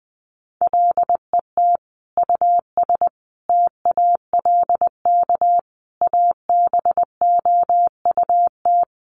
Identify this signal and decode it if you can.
Morse code, heard as a keyed tone, about 20 wpm, 700 Hz. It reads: LET US TALK ABOUT
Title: Sentences from Top 400 Words 20wpm